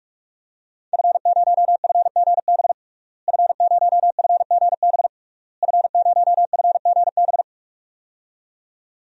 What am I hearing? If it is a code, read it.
Morse code, heard as a keyed tone, about 41 wpm, 700 Hz. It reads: F0FGB F0FGB F0FGB
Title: DXCC-100 Call Signs - 5 Characters - Repeated 3x - 45wpm